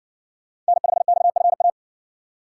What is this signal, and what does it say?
Morse code, heard as a keyed tone, about 58 wpm, 700 Hz. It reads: N5XFA